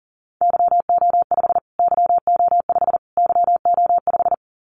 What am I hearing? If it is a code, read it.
Morse code, heard as a keyed tone, about 40 wpm, 700 Hz. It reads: YO5 YO5 YO5
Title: Callsign Prefixes for Top 25 DXCC - Repeated 3x - 40wpm